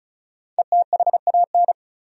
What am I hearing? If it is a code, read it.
Morse code, heard as a keyed tone, about 34 wpm, 700 Hz. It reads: ETHAN
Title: USA Common Names - Max 5 Letters - 35wpm